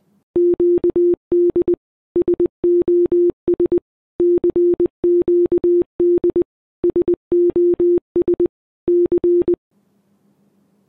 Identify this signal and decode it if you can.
Morse code, heard as a keyed tone, about 20 wpm, 350 Hz. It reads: QD SOS CQD SOS C